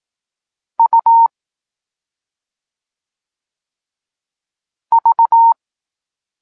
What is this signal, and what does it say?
Morse code, heard as a keyed tone, about 18 wpm, spaced as for 2 wpm, 920 Hz. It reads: UV
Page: Tekst 8 – doskonalenie odbioru poznanych liter